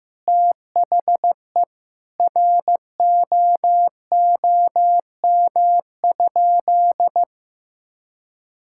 Morse code, heard as a keyed tone, about 15 wpm, 700 Hz. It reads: THE ROOM?
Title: Sentences from Top 1000 Words - Review Part 5 - 15wpm